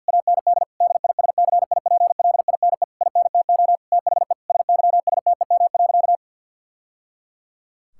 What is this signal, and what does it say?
Morse code, heard as a keyed tone, about 50 wpm, 700 Hz. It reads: AND DISCIPLINE INTO THE SYSTEM.
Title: Sentences from Top 1000 Words - Review Part 6 - 50wpm